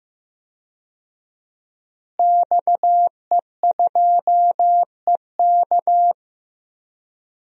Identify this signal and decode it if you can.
Morse code, heard as a keyed tone, about 15 wpm, 700 Hz. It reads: XE2EK
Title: DXCC-100 Call Signs - 5 Characters - Repeated 3x - 15wpm